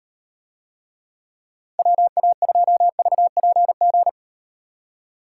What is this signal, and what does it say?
Morse code, heard as a keyed tone, about 38 wpm, 700 Hz. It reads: WA2VPG